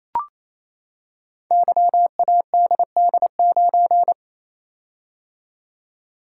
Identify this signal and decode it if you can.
Morse code, heard as a keyed tone, about 29 wpm, 700 Hz. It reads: YADD9
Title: Code Groups - 5 Letter:Numbers - 28wpm